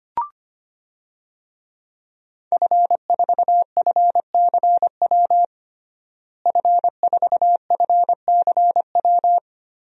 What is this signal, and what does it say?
Morse code, heard as a keyed tone, about 25 wpm, 700 Hz. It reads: F4FCW F4FCW
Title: DXCC-100 Call Signs - 5 Characters - Repeated 3x - 25wpm